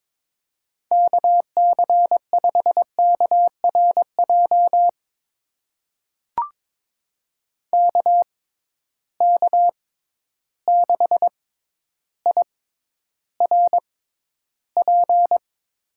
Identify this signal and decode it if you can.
Morse code, heard as a keyed tone, about 22 wpm, spaced as for 11 wpm, 700 Hz. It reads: KC5KRJ KK6IRP
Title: USA Call Signs - 6 Characters - 6x Character Spacing - 22wpm